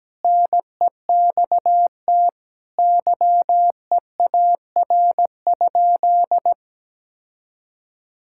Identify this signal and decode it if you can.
Morse code, heard as a keyed tone, about 17 wpm, 700 Hz. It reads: NEXT YEAR?
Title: Sentences from Top 1000 Words - Review Part 5 - 17wpm